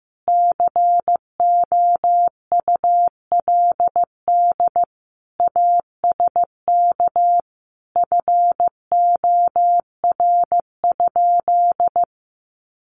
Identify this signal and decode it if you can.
Morse code, heard as a keyed tone, about 15 wpm, 700 Hz. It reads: COULD ASK FOR?
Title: Sentences from Top 1400 Words 15wpm